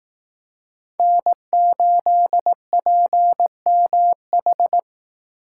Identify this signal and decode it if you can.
Morse code, heard as a keyed tone, about 18 wpm, 700 Hz. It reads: N8PMH
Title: Speed Racing - USA 1x3 Call Signs 15wpm